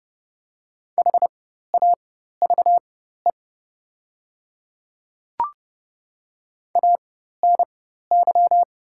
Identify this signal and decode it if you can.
Morse code, heard as a keyed tone, about 31 wpm, spaced as for 14 wpm, 700 Hz. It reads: HAVE ANY
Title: 4x Character Spacing - Top 500 Words - 30wpm